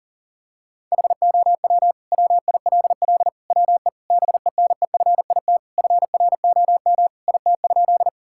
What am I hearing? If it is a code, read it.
Morse code, heard as a keyed tone, about 37 wpm, 700 Hz. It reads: HOW WILL WE BENEFIT FROM IT?
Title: Sentences from Top 1000 Words - Review Part 6 - 40wpm